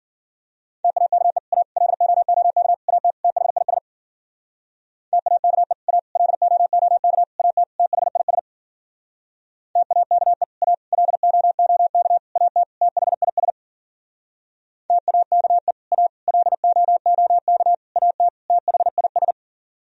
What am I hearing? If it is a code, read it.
Morse code, heard as a keyed tone, about 51 wpm, 700 Hz. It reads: TAKE A LOOK AT THIS TAKE A LOOK AT THIS TAKE A LOOK AT THIS TAKE A LOOK AT THIS